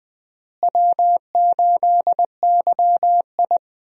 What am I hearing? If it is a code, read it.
Morse code, heard as a keyed tone, about 20 wpm, 700 Hz. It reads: W8YI
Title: USA Call Signs - 4 Characters - Repeated 3x - 20wpm